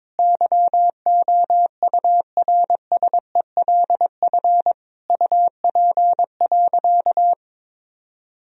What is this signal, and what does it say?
Morse code, heard as a keyed tone, about 22 wpm, 700 Hz. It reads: YOURSELF UP.
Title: Sentences from Top 1000 Words - Review Part 1 - 22wpm